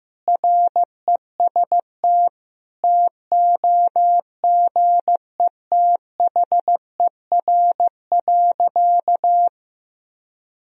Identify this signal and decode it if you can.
Morse code, heard as a keyed tone, about 15 wpm, 700 Hz. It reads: REST TOGETHER.
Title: Sentences from Top 500 Words and Abbreviations 15wpm